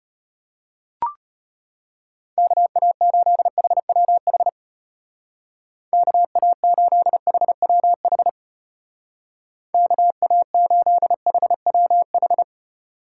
Morse code, heard as a keyed tone, about 31 wpm, 700 Hz. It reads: KA8HWH KA8HWH KA8HWH
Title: Speed Racing - USA 2x3 Call Signs 25wpm